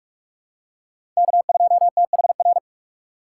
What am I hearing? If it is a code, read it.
Morse code, heard as a keyed tone, about 46 wpm, 700 Hz. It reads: K2THR
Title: USA Call Signs - 5 Characters - 3x Character Spacing - 45wpm